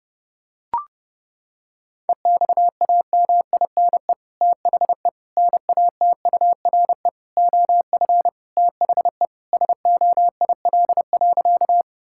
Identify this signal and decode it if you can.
Morse code, heard as a keyed tone, about 32 wpm, 700 Hz. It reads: EXAMINE THE NATURE OF THE SOIL.
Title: Sentences from Top 1600 Words 30wpm